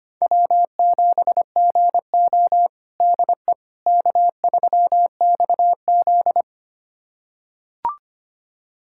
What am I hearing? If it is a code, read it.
Morse code, heard as a keyed tone, about 25 wpm, 700 Hz. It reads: W7GO DE K3XZ
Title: USA Callsign DE Callsign - 4 Characters - 25wpm